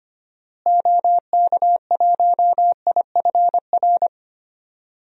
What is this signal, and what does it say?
Morse code, heard as a keyed tone, about 25 wpm, 700 Hz. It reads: OK1IFR
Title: DXCC-100 Call Signs - 6 Characters - Repeated 3x - 25wpm